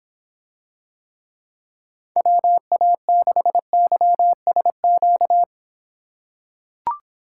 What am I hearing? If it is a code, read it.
Morse code, heard as a keyed tone, about 26 wpm, 700 Hz. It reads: WA6YSQ